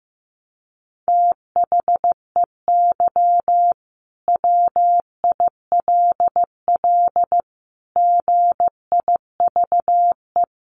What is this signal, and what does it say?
Morse code, heard as a keyed tone, about 14 wpm, 700 Hz. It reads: THEY WILL GIVE
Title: Sentences from Top 600 Words 15wpm